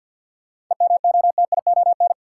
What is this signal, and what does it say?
Morse code, heard as a keyed tone, about 53 wpm, 700 Hz. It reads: EMOTION
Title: Sentences from Top 1600 Words - New Word Drill - 50wpm